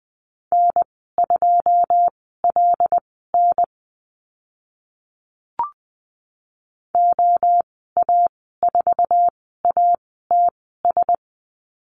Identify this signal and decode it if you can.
Morse code, heard as a keyed tone, about 20 wpm, spaced as for 15 wpm, 700 Hz. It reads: N2LN OA4ATS
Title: 088 Lesson 15g - Number 5 - Call Signs 20wpm